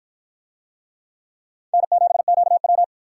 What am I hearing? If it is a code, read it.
Morse code, heard as a keyed tone, about 49 wpm, 700 Hz. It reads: N7QW